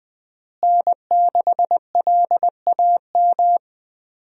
Morse code, heard as a keyed tone, about 21 wpm, 700 Hz. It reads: N6LAM
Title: USA Call Signs - 5 Characters - Repeated 3x - 20wpm